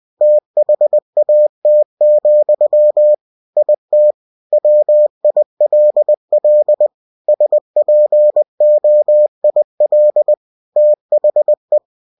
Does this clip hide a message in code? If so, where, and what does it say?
Yes: Morse code from 0.2 to 11.8 s, heard as a keyed tone, about 20 wpm, 600 Hz. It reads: THAT, IT WILL SPOIL THE